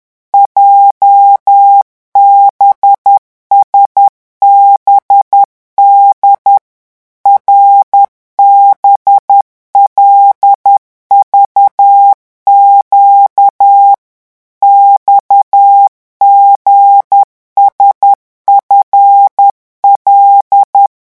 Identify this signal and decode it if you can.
Morse code, heard as a keyed tone, about 10 wpm, 800 Hz. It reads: JBSBD RBLVQ XGSFL